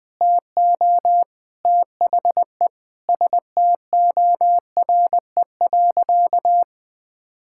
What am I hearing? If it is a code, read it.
Morse code, heard as a keyed tone, about 20 wpm, 700 Hz. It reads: TO THE STORE.
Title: Sentences with 4-Word Phrases - Spoken Beginning - 20wpm